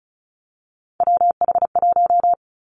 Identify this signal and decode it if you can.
Morse code, heard as a keyed tone, about 36 wpm, 700 Hz. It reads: WH1
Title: Three Challenging Letter-Number Combinations 35wpm